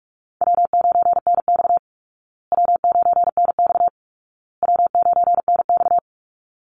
Morse code, heard as a keyed tone, about 44 wpm, 700 Hz. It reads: W9NX W9NX W9NX
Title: Easy Call Signs Repeated 3x - 45wpm